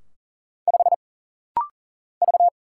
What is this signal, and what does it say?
Morse code, heard as a keyed tone, about 40 wpm, 700 Hz. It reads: 5 V
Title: Single Letter-Number - Rapid-Fire 40wpm